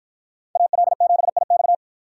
Morse code, heard as a keyed tone, about 53 wpm, 700 Hz. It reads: AL7IX